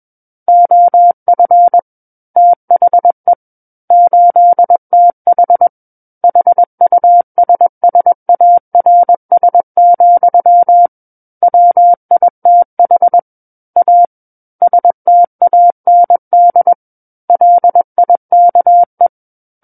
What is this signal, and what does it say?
Morse code, heard as a keyed tone, about 21 wpm, 700 Hz. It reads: OF THE 8TH HUSSARS, WITH A STAND LIKE
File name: Book One / 12-Chapter 12 - 21 WPM